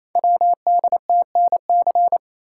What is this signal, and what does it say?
Morse code, heard as a keyed tone, about 28 wpm, 700 Hz. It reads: WDTNC
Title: Code Groups - 5 Letters - 28wpm